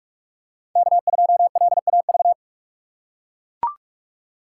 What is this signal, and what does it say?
Morse code, heard as a keyed tone, about 44 wpm, 700 Hz. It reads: K2LAV